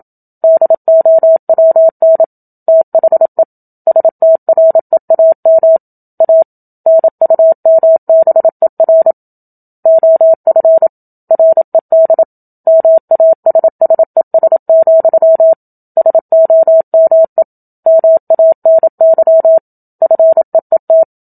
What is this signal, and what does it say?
Morse code, heard as a keyed tone, about 27 wpm, 650 Hz. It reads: DOWN THE STREAM A NUMBER OF RED MASSES, SOME MANY FEET
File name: Book Two / 01-Chapter 1 - 27 WPM